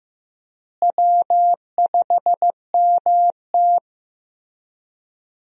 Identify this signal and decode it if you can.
Morse code, heard as a keyed tone, about 15 wpm, 700 Hz. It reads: W5MT
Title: USA Call Signs - 4 Characters - Repeated 3x - 15wpm